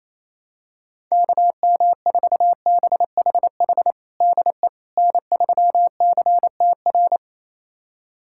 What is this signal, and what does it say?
Morse code, heard as a keyed tone, about 28 wpm, 700 Hz. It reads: KM4BHH DE N3CTR
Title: USA Callsign DE Callsign 28wpm